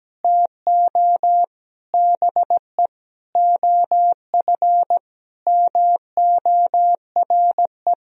Morse code, heard as a keyed tone, about 17 wpm, 700 Hz. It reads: TO BE OF MORE
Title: Sentences from Top 1000 Words - Review Part 2 - 17wpm